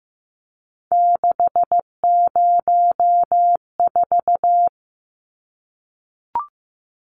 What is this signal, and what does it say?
Morse code, heard as a keyed tone, about 15 wpm, 700 Hz. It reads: 604
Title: Numbers - 3-Digit 15wpm